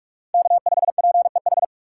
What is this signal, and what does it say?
Morse code, heard as a keyed tone, about 46 wpm, 690 Hz. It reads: K5PEH
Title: USA Call Signs - 4x Character Spacing - 45wpm